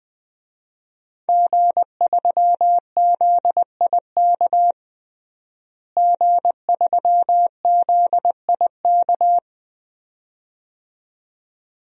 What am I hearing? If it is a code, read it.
Morse code, heard as a keyed tone, about 20 wpm, 700 Hz. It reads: G3ZIK G3ZIK
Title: DXCC-100 Call Signs - 5 Characters - Repeated 2x - 20wpm